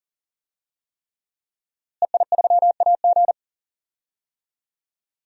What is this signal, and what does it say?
Morse code, heard as a keyed tone, about 40 wpm, 700 Hz. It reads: EI3AG
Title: DXCC-100 Call Signs - 5 Characters - Repeated 3x - 40wpm